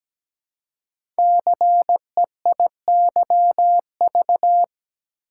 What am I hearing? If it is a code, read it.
Morse code, heard as a keyed tone, about 17 wpm, 700 Hz. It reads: CEIYV
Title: Code Groups - 5 Letter:Numbers - 17wpm